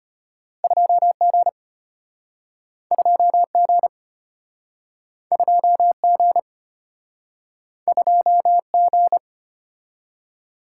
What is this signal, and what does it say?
Morse code, heard as a keyed tone, about 32 wpm, 700 Hz. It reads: 2G 2G 2G 2G